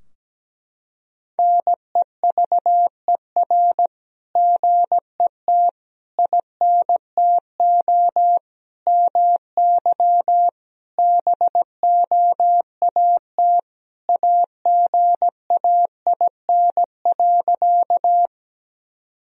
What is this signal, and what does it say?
Morse code, heard as a keyed tone, about 17 wpm, 700 Hz. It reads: NEVER GET INTO MY BOAT AGAIN.
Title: Sentences from Top 1000 Words - Review Part 3 - 17wpm